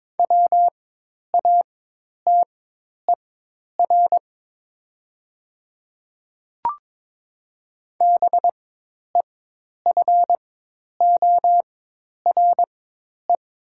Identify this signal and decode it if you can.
Morse code, heard as a keyed tone, about 22 wpm, spaced as for 10 wpm, 700 Hz. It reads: WATER BEFORE
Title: 4x Character Spacing - Top 200 Words - 22wpm